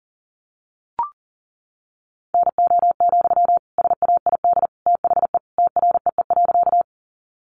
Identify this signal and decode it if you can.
Morse code, heard as a keyed tone, about 40 wpm, 700 Hz. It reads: NO, SAID THE TREE.
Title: Sentences from Top 300 Words 40wpm